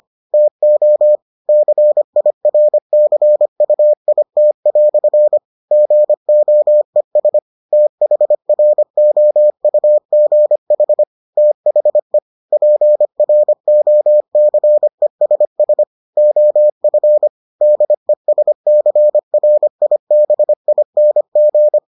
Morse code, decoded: TO CIRCUIT" GOES THROUGH THE PROCESS OF DESCRIBING

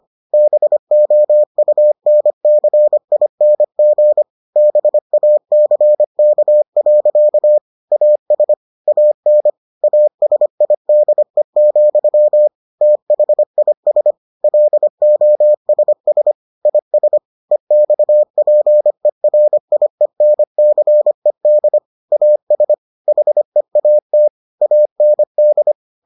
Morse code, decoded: BOUNCING BACK. AS AN ASIDE, THIS LOSS IS EXPERIENCED AS HEAT AND

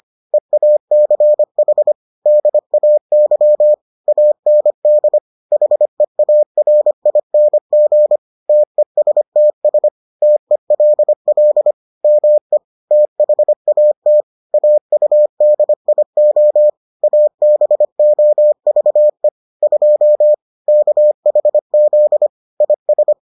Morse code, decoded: EACH DAY AND HEARING TESTS TELL ME THAT AUDIO ABOVE 2 KHZ IS

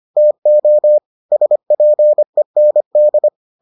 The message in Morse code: TO SPEND

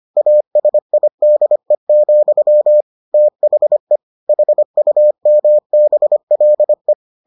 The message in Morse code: ASIDE, THE HUMBLE

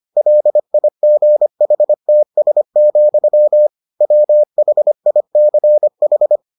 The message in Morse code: LIGHTS, WHICH

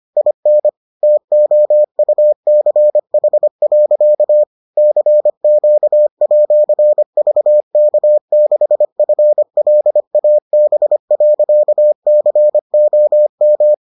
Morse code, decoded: IN TOUCH. CQ@VK6FLAB.COM

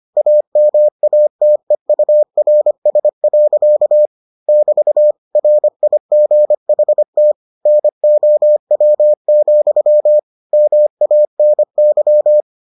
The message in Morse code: AMATEURS. <BT> RIGHT NOW, MANY

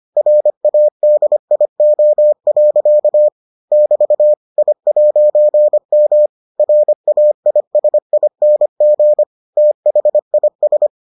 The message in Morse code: RADIO. <BT> I'M RAISING THIS

